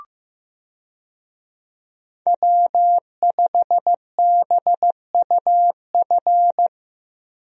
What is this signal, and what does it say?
Morse code, heard as a keyed tone, about 16 wpm, 700 Hz. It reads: W5BUF
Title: USA Call Signs - 5 Characters - Repeated 3x - 15wpm